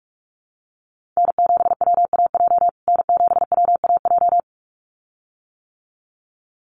Morse code, heard as a keyed tone, about 44 wpm, 700 Hz. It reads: N7WAJ N7WAJ
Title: USA Call Signs Repeated 2x 45wpm